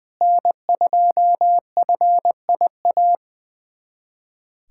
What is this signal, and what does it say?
Morse code, heard as a keyed tone, about 21 wpm, 700 Hz. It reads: N2FIA
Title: USA Callsign DE Callsign - 5 Characters - 20wpm